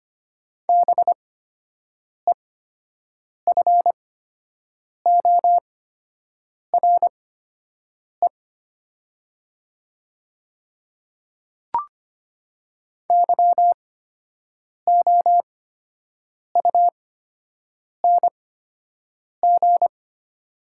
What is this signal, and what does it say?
Morse code, heard as a keyed tone, about 25 wpm, spaced as for 7 wpm, 700 Hz. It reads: BEFORE YOUNG
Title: 8x Character Spacing - Top 500 Words - 25wpm